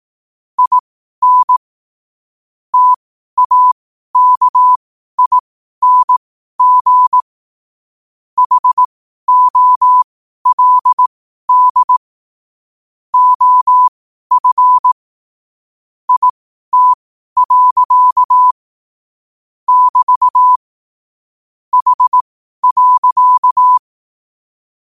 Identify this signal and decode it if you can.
Morse code, heard as a keyed tone, about 18 wpm, spaced as for 12 wpm, 1000 Hz. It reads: IN TAKING HOLD OF IT. <BT> H.
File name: qotdfiles / 12wpm